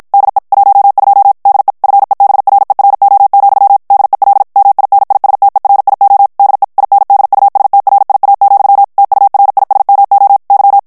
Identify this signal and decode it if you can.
Morse code, heard as a keyed tone, about 52 wpm, 800 Hz. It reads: DE 12 DE FEBRERO, DEL MINISTERIO DE INDUSTRIA, TURISMO Y
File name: morse mp3 / ESPAGNOL TEXTE EN CLAIR 2 VITESSE 50 MOTS MINUTES